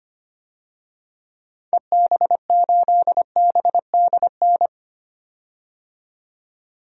Morse code, heard as a keyed tone, about 25 wpm, 700 Hz. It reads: EB8BDN
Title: DXCC-100 Call Signs - 6 Characters - 25wpm